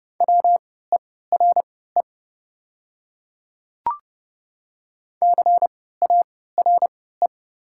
Morse code, heard as a keyed tone, about 32 wpm, spaced as for 17 wpm, 690 Hz. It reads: WERE CARE
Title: 3x Character Spacing - Top 300 Words - 30wpm